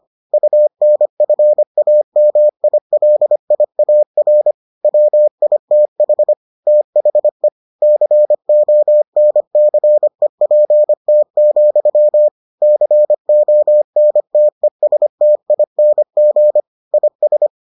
Morse code, heard as a keyed tone, about 25 wpm, 600 Hz. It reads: UNFAMILIAR WITH THE CONCEPT, CONTESTING IS